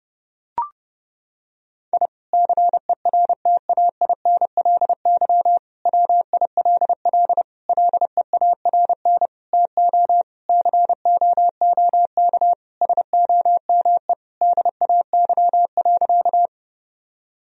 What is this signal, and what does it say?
Morse code, heard as a keyed tone, about 32 wpm, 700 Hz. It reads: I CERTAINLY WILL LEARN TO COOK SOME DAY.
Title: Sentences from Top 1000 Words - Review Part 4 - 30wpm